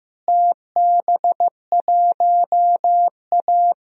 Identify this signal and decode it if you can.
Morse code, heard as a keyed tone, about 16 wpm, 700 Hz. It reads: TB1A
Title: DXCC-100 Call Signs - 4 Characters - Repeated 3x - 15wpm